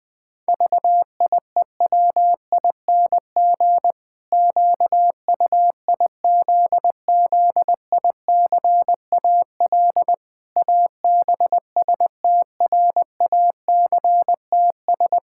Morse code, heard as a keyed tone, about 20 wpm, 700 Hz. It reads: VIEWING QUIZZICAL ABSTRACTS
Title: Pangram Challenge v2 - 20wpm